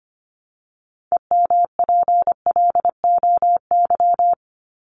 Morse code, heard as a keyed tone, about 25 wpm, 700 Hz. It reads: EMPLOY
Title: ED Suffix - Top 100 Words with Base Words 25wpm